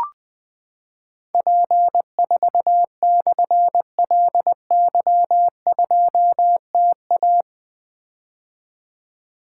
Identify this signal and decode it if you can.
Morse code, heard as a keyed tone, about 20 wpm, 700 Hz. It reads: P4/LY2TA
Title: Difficult Call Signs 20wpm